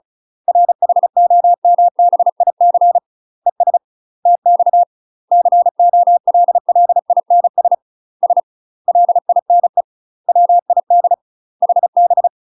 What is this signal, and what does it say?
Morse code, heard as a keyed tone, about 34 wpm, 700 Hz. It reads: RHOMBIC ES TX COLLINS S LINE WID HB
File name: UK MASTER QSOs TRAINING / 35WPM / UK MASTER QSOs 0000